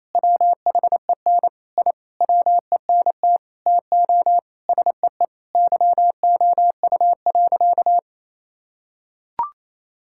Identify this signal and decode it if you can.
Morse code, heard as a keyed tone, about 28 wpm, 700 Hz. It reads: WHEN I WENT TO SEE YOU.
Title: Sentences from Top 1000 Words - Review Part 5 - 28wpm